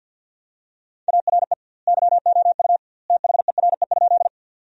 Morse code, heard as a keyed tone, about 47 wpm, 700 Hz. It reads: ARE YOU THERE?
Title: 2x Word-Spacing - Sentences from Top 100 Words - 50wpm